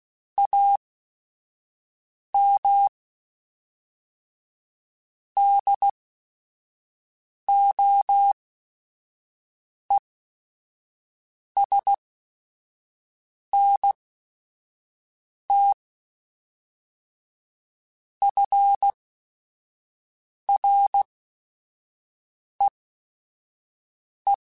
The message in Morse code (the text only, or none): AM DOESNT FREE